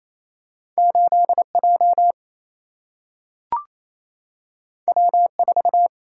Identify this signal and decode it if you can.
Morse code, heard as a keyed tone, about 28 wpm, 700 Hz. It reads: 8J W4
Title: Single-Letter Single-Number 28wpm